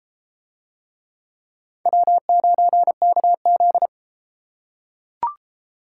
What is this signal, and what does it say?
Morse code, heard as a keyed tone, about 30 wpm, 700 Hz. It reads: W9KZ